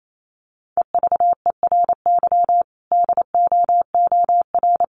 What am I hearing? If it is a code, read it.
Morse code, heard as a keyed tone, about 28 wpm, 700 Hz. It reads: EVERY DOOR
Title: Sentences from Top 1400 Words 28wpm